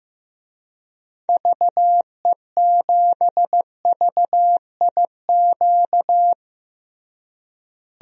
Morse code, heard as a keyed tone, about 16 wpm, 700 Hz. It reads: VE7VIQ
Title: DXCC-100 Call Signs - 6 Characters - Repeated 2x - 15wpm